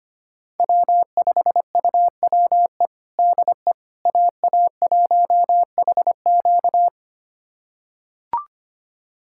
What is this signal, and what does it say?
Morse code, heard as a keyed tone, about 25 wpm, 700 Hz. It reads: W5UWE DE AA1HQ
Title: USA Callsign DE Callsign - 5 Characters - 25wpm